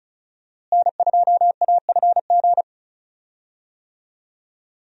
Morse code, heard as a keyed tone, about 36 wpm, 700 Hz. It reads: N2AFG
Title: USA Call Signs - 5 Characters - Repeated 3x - 35wpm